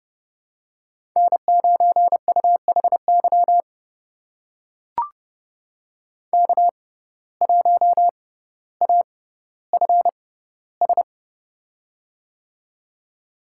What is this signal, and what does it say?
Morse code, heard as a keyed tone, about 30 wpm, spaced as for 15 wpm, 700 Hz. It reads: N9UHY K1AFS
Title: USA Call Signs - 5 Characters - 6x Character Spacing - 30wpm